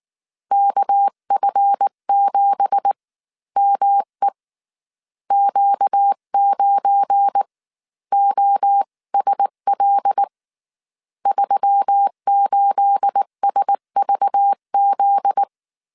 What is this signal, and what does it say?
Morse code, heard as a keyed tone, about 19 wpm, spaced as for 15 wpm, 800 Hz. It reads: KF7ME Q9OSL 38SVZ
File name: CW / DET / 1 9 1500C S B 18wpm 2551